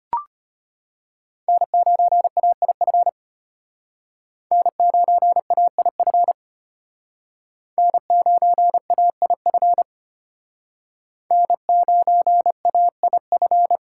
Morse code, heard as a keyed tone, about 33 wpm, 700 Hz. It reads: N9AIF N9AIF N9AIF N9AIF